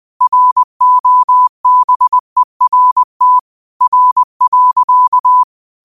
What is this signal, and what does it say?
Morse code, heard as a keyed tone, about 20 wpm, 1000 Hz. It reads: ROBERT R.
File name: qotdfiles / 20wpm